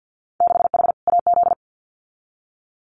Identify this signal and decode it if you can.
Morse code, heard as a keyed tone, about 50 wpm, 700 Hz. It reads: 6H AZ
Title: Winter Field Day Exchanges Encore 50wpm